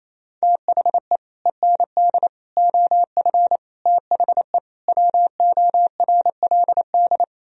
Morse code, heard as a keyed tone, about 37 wpm, 700 Hz. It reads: THE END OF THE WORLD